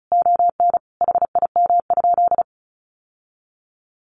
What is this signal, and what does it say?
Morse code, heard as a keyed tone, about 34 wpm, 700 Hz. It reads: ON HIM?
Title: Sentences from Top 1100 Words 35wpm